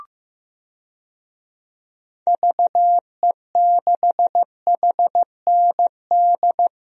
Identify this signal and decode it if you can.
Morse code, heard as a keyed tone, about 16 wpm, 700 Hz. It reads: VE6HND
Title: DXCC-100 Call Signs - 6 Characters - Repeated 2x - 15wpm